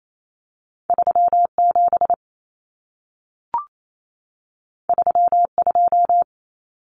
Morse code, heard as a keyed tone, about 28 wpm, 700 Hz. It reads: 37 32
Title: Common Cut Numbers - 2-Digit 28wpm